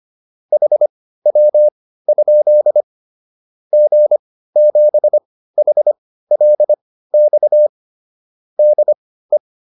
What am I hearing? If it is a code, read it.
Morse code, heard as a keyed tone, about 25 wpm, spaced as for 15 wpm, 600 Hz. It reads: HW? G7HLX DE